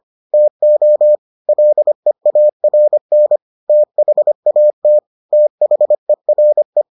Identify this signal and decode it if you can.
Morse code, heard as a keyed tone, about 25 wpm, 600 Hz. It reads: TO LEARN THAT THERE